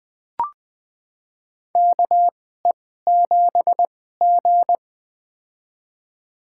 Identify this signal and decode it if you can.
Morse code, heard as a keyed tone, about 21 wpm, spaced as for 15 wpm, 700 Hz. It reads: KE7G
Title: USA Call Signs - 4 Characters - 2x Character Spacing - 20wpm